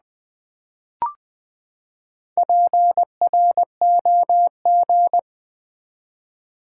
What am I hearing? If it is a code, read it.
Morse code, heard as a keyed tone, about 20 wpm, 700 Hz. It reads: PROG
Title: Build A Word - 4 Letters At A Time - 11-Letter Words - 20wpm